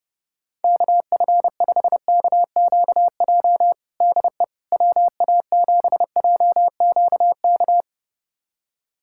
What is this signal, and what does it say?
Morse code, heard as a keyed tone, about 30 wpm, 700 Hz. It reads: KF5KQJ DE WA7JQK
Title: USA Callsign DE Callsign 30wpm